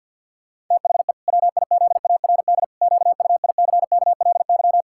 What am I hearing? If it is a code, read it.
Morse code, heard as a keyed tone, about 50 wpm, 700 Hz. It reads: THE WIZARD QUICKLY
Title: Pangram Challenge v2 - 50wpm